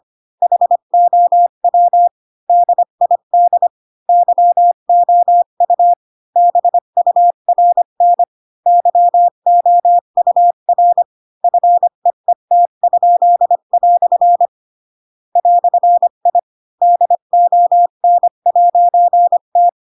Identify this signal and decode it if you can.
Morse code, heard as a keyed tone, about 25 wpm, 700 Hz. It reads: HOW DID YOU BURN YOUR FEET?" "I DON'T
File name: Pinocchio / Pinocchio ch7 20 25 ik1hge